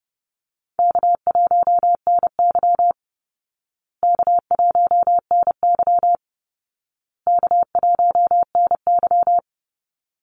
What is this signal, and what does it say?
Morse code, heard as a keyed tone, about 30 wpm, 700 Hz. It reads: K1NY K1NY K1NY
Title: Easy Call Signs Repeated 3x - 30wpm